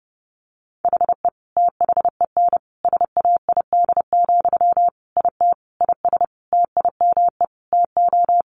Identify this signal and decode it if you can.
Morse code, heard as a keyed tone, about 30 wpm, 700 Hz. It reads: HE THEN SAID, IT IS TIME TO
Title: Sentences from Top 300 Words 30wpm